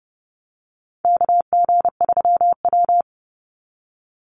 Morse code, heard as a keyed tone, about 32 wpm, 700 Hz. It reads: KG3W
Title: NA SOTA Chaser Callsigns 30wpm